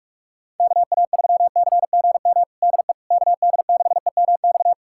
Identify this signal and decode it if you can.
Morse code, heard as a keyed tone, about 46 wpm, 690 Hz. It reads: KA3CGM DE KD6EMX
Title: USA Callsign DE Callsign 45wpm